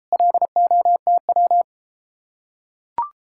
Morse code, heard as a keyed tone, about 33 wpm, 700 Hz. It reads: LOTW